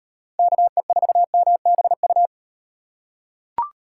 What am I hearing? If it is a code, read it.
Morse code, heard as a keyed tone, about 38 wpm, 700 Hz. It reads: KE4MBU